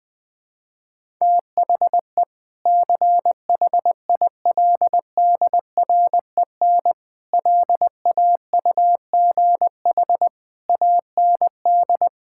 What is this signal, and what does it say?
Morse code, heard as a keyed tone, about 20 wpm, 700 Hz. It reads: THE CHILDREN LAUGH AND
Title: Sentences with 2-Word Phrases - Spoken Beginning - 20wpm